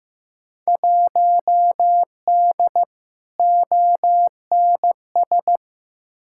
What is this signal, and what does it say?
Morse code, heard as a keyed tone, about 15 wpm, 700 Hz. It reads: 1D ONS
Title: Field Day Exchanges - 15wpm